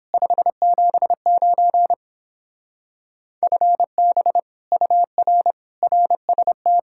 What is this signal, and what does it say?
Morse code, heard as a keyed tone, about 29 wpm, 700 Hz. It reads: 579 FB UR RST